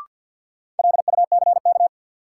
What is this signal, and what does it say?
Morse code, heard as a keyed tone, about 53 wpm, 700 Hz. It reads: LUCK
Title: Sentences from Top 1500 Words - New Word Drill - 50wpm